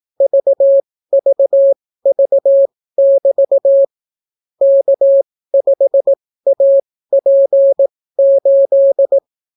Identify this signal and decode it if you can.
Morse code, heard as a keyed tone, about 18 wpm, spaced as for 15 wpm, 550 Hz. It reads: VVV<BT> K5AP8